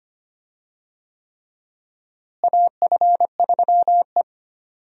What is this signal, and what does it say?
Morse code, heard as a keyed tone, about 25 wpm, 700 Hz. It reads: AF3E
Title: USA Call Signs - 4 Characters - Repeated 3x - 25wpm